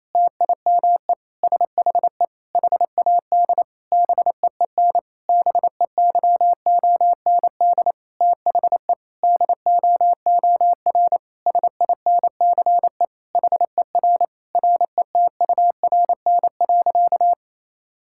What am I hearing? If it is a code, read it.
Morse code, heard as a keyed tone, about 28 wpm, 700 Hz. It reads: TIME SHE HAD BEEN BEYOND THE DOOR SINCE HER RETURN.
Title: Sentences from Top 1000 Words - Review Part 6 - 28wpm